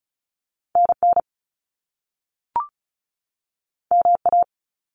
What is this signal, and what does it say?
Morse code, heard as a keyed tone, about 36 wpm, 700 Hz. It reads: NN MA
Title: Top Two Letter Combinations in Words 35wpm